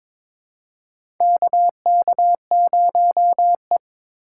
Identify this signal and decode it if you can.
Morse code, heard as a keyed tone, about 21 wpm, 700 Hz. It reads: KK0E
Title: USA Call Signs - 3x Character Spacing - 22wpm